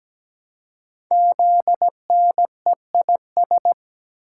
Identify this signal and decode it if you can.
Morse code, heard as a keyed tone, about 17 wpm, 700 Hz. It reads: ZNEIS
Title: Code Groups - 5 Letters - 17wpm